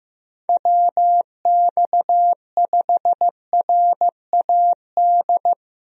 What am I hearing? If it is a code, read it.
Morse code, heard as a keyed tone, about 15 wpm, 700 Hz. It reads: WX5RAD
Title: USA Call Signs - 6 Characters - Repeated 3x - 15wpm